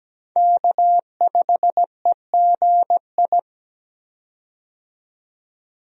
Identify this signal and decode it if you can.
Morse code, heard as a keyed tone, about 17 wpm, 700 Hz. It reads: K5EGI
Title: USA Call Signs - 5 Characters - 17wpm